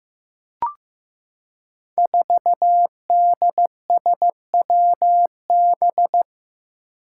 Morse code, heard as a keyed tone, about 15 wpm, 700 Hz. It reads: 4DSWB
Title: Five Challenging Letter-Number Combinations 15wpm